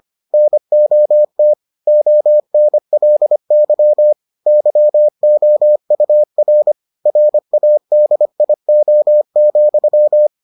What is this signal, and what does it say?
Morse code, heard as a keyed tone, about 25 wpm, 600 Hz. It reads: NOT ONLY YOUR RADIO,